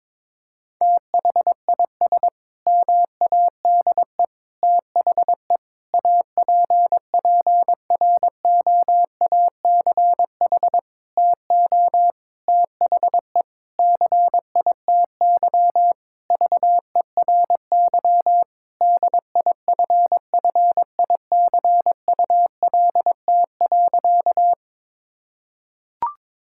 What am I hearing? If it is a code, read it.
Morse code, heard as a keyed tone, about 22 wpm, 700 Hz. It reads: THIS MADE THE APPROACH TO THE CITY VERY DIFFICULT.
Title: Sentences from Top 1000 Words - Review Part 6 - 22wpm